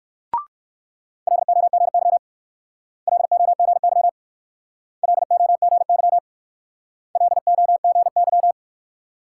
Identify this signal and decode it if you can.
Morse code, heard as a keyed tone, about 56 wpm, 690 Hz. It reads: LOGY LOGY LOGY LOGY
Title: Speed-Racing Multi-letter Phonemes - 45wpm